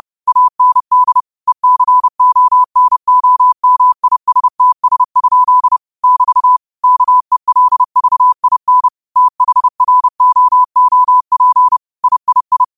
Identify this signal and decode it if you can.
Morse code, heard as a keyed tone, about 30 wpm, 1000 Hz. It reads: AND ECONOMISTS? <BT> KELVIN THROOP III
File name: qotdfiles / 30wpm